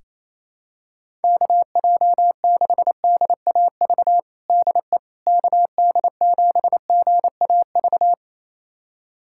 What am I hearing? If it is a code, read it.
Morse code, heard as a keyed tone, about 28 wpm, 700 Hz. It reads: KJ6DAV DE KD7GAV
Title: USA Callsign DE Callsign - 6 Characters - 28wpm